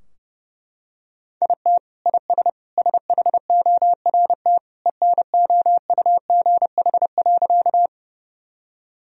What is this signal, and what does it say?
Morse code, heard as a keyed tone, about 32 wpm, 700 Hz. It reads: IT IS SHORT ENOUGH.
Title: Sentences from Top 1000 Words - Review Part 2 - 30wpm